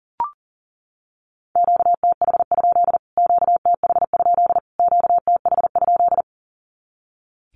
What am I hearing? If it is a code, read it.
Morse code, heard as a keyed tone, about 37 wpm, 700 Hz. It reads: QTH? QTH? QTH?
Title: Q-Codes Repeated 3x 40wpm